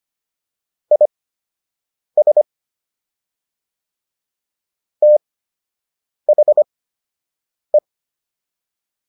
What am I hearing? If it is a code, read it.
Morse code, heard as a keyed tone, about 25 wpm, spaced as for 7 wpm, 600 Hz. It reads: IS THE